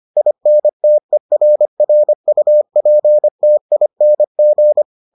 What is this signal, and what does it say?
Morse code, heard as a keyed tone, about 25 wpm, 600 Hz. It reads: INTERRUPTING